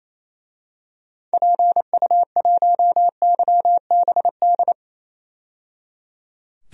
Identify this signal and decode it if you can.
Morse code, heard as a keyed tone, about 28 wpm, 700 Hz. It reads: PU1YBD
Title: DXCC-100 Call Signs - 6 Characters - 28wpm